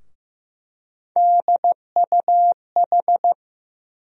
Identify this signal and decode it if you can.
Morse code, heard as a keyed tone, about 15 wpm, 700 Hz. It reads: DUH
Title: USA Call Sign Postfixes - 3 Letters 15wpm